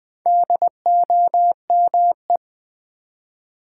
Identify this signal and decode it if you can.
Morse code, heard as a keyed tone, about 21 wpm, 700 Hz. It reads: DOME
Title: Build A Word - 4 Letters At A Time - 8-Letter Words - 20wpm